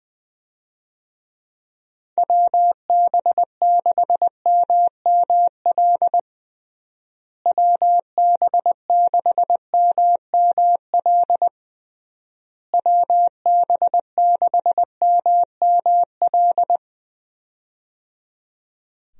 Morse code, heard as a keyed tone, about 20 wpm, 700 Hz. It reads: WB6MML WB6MML WB6MML
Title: USA Call Signs - 6 Characters - Repeated 3x - 20wpm